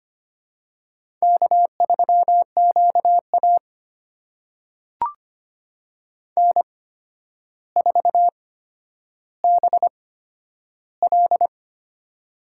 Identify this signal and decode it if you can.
Morse code, heard as a keyed tone, about 25 wpm, spaced as for 11 wpm, 700 Hz. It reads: K3QA N4BL
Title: USA Call Signs - 4 Characters - 8x Character Spacing - 25wpm